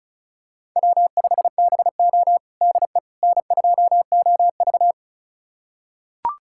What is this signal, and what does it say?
Morse code, heard as a keyed tone, about 34 wpm, 700 Hz. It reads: W5BO DE N2OV
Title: USA Callsign DE Callsign - 4 Characters - 35wpm